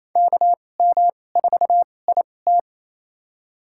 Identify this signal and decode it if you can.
Morse code, heard as a keyed tone, about 28 wpm, spaced as for 20 wpm, 700 Hz. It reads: KM4IT
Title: USA Call Signs - 5 Characters - 2x Character Spacing - 28wpm